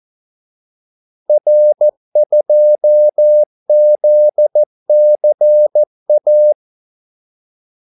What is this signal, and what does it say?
Morse code, heard as a keyed tone, about 14 wpm, 600 Hz. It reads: R2ZCA
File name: Callsigns-14wpm